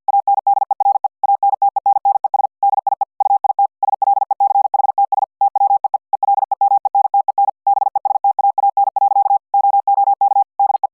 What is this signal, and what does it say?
Morse code, heard as a keyed tone, about 50 wpm, 800 Hz. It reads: ANDERE ANTENNES DIE UIT SLECHTS TWEE ELEMENTEN BESTAAN. OOK DE